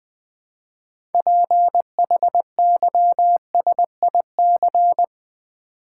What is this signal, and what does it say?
Morse code, heard as a keyed tone, about 20 wpm, 700 Hz. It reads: PHYSIC
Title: Build A Word - 2 Letters At A Time - 8-Letter Words - 20wpm